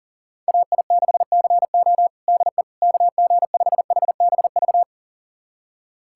Morse code, heard as a keyed tone, about 40 wpm, 700 Hz. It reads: AI6CO DE KG5HBV
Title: USA Callsign DE Callsign 40wpm